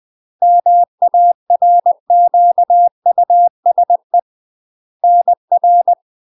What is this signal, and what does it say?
Morse code, heard as a keyed tone, about 20 wpm, 700 Hz. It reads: MARQUSE NR
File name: UK MASTER QSOs TRAINING / 20WPM / UK MASTER QSOs 0059